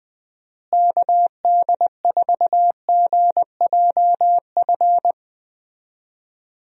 Morse code, heard as a keyed tone, about 20 wpm, 700 Hz. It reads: KD4GJF
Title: USA Call Signs - 6 Characters - Repeated 3x - 20wpm